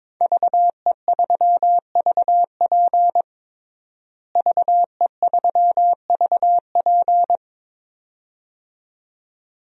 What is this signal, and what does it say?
Morse code, heard as a keyed tone, about 22 wpm, 700 Hz. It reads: VE3VP VE3VP
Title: DXCC-100 Call Signs - 5 Characters - Repeated 2x - 22wpm